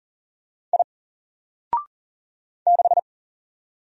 Morse code, heard as a keyed tone, about 40 wpm, 700 Hz. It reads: I 6
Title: Single Letter-Number 40wpm